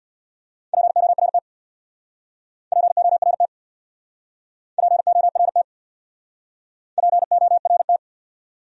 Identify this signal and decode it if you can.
Morse code, heard as a keyed tone, about 64 wpm, 700 Hz. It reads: PORT PORT PORT PORT